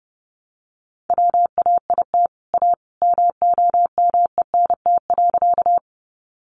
Morse code, heard as a keyed tone, about 30 wpm, 700 Hz. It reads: WAIT A MOMENT.
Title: Sentences from Top 1000 Words 30wpm